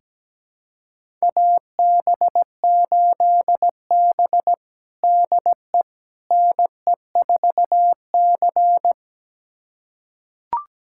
Morse code, heard as a keyed tone, about 17 wpm, 700 Hz. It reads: AB8B DE NE4C
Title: USA Callsign DE Callsign - 4 Characters - 17wpm